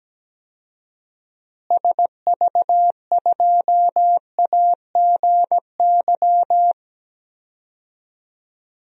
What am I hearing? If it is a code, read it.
Morse code, heard as a keyed tone, about 17 wpm, 700 Hz. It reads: SV2AGY
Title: DXCC-100 Call Signs - 6 Characters - Repeated 2x - 17wpm